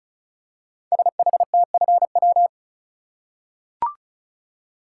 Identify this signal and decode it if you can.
Morse code, heard as a keyed tone, about 36 wpm, 700 Hz. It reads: SHTFW
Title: Code Groups - 5 Letters - 35wpm